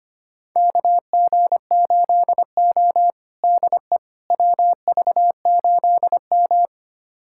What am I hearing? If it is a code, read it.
Morse code, heard as a keyed tone, about 25 wpm, 700 Hz. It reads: KG8O DE WV8M
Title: USA Callsign DE Callsign - 4 Characters - 25wpm